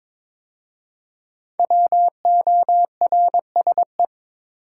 Morse code, heard as a keyed tone, about 22 wpm, 700 Hz. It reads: WORSE
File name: 5 Letter Words - First 3 Letters Given - 22wpm